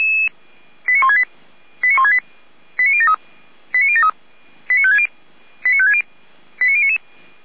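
Five-tone selective call (ZVEI), heard as quick runs of five tones, tones of about 70 ms. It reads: 89168 89168 89073 89073 89560 89560 89099
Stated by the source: ZVEI five-tone selective call received on an SDR.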